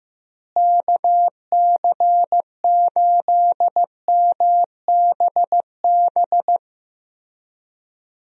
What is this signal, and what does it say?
Morse code, heard as a keyed tone, about 15 wpm, 700 Hz. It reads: KC8MBB
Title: USA Call Signs - 6 Characters - Repeated 2x - 15wpm